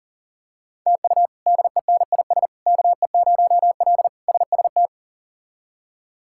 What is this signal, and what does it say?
Morse code, heard as a keyed tone, about 40 wpm, 700 Hz. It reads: TU DENIS KE0L SST
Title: K1USN SST QSO Practice - 40wpm